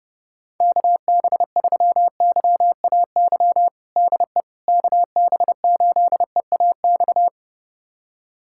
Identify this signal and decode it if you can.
Morse code, heard as a keyed tone, about 32 wpm, 700 Hz. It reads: KB3YAY DE KB8EAX
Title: USA Callsign DE Callsign - 6 Characters - 30wpm